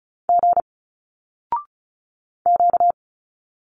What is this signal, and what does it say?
Morse code, heard as a keyed tone, about 35 wpm, 700 Hz. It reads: G Q
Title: Single Letters 35wpm